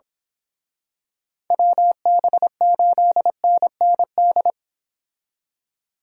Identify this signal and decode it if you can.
Morse code, heard as a keyed tone, about 26 wpm, 700 Hz. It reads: WB8NND
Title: Speed Racing - USA Call Signs 22wpm